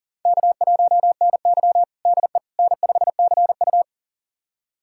Morse code, heard as a keyed tone, about 40 wpm, 700 Hz. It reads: K1NY DE N5CU
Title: USA Callsign DE Callsign - 4 Characters - 40wpm